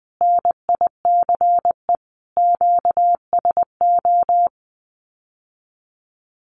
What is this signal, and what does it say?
Morse code, heard as a keyed tone, about 19 wpm, 700 Hz. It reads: NICE QSO
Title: Common QSO Phrases Encore 20wpm